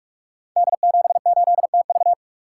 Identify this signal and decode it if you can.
Morse code, heard as a keyed tone, about 44 wpm, 700 Hz. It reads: D78TV
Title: Code Groups - 5 Letter:Numbers - 45wpm